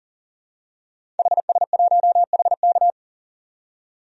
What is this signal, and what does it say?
Morse code, heard as a keyed tone, about 40 wpm, 700 Hz. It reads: HS1HK
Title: DXCC-100 Call Signs - 5 Characters - 40wpm